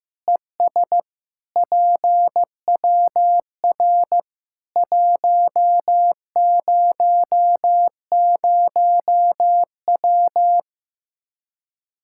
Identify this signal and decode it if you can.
Morse code, heard as a keyed tone, about 15 wpm, 700 Hz. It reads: ES PWR 100W
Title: Practice QSO - Exchange 2 - Info - 15wpm